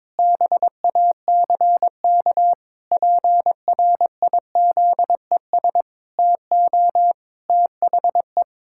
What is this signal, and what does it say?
Morse code, heard as a keyed tone, about 22 wpm, 700 Hz. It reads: BACK PRIZES TO THE
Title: Pangram Challenge v2 - 22wpm